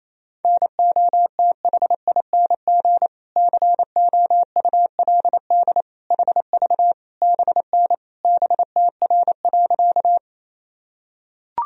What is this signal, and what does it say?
Morse code, heard as a keyed tone, about 28 wpm, 700 Hz. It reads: NOTHING COULD HV BN BTR.
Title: Sentences from Top 500 Words and Abbreviations 28wpm